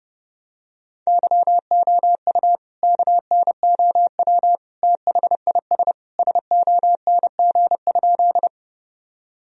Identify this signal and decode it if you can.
Morse code, heard as a keyed tone, about 30 wpm, 700 Hz. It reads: YOU KNOW THIS SONG?
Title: Sentences from Top 1000 Words - Review Part 2 - 30wpm